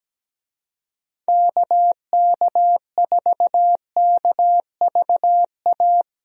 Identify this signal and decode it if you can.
Morse code, heard as a keyed tone, about 17 wpm, 700 Hz. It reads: KK4KVA
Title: USA Call Signs - 6 Characters - Repeated 2x - 17wpm